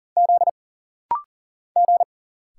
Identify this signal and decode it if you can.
Morse code, heard as a keyed tone, about 39 wpm, 700 Hz. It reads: Z G
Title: Single Letter-Number Alt - Rapid-Fire 40wpm